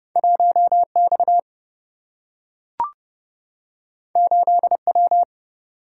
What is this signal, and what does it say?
Morse code, heard as a keyed tone, about 30 wpm, 700 Hz. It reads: 1X 8W
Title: Single-Letter Single-Number 30wpm